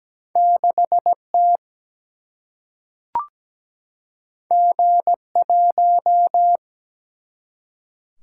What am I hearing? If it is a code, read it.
Morse code, heard as a keyed tone, about 17 wpm, 700 Hz. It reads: 6 T G 1
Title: Single-Letter Single-Number 17wpm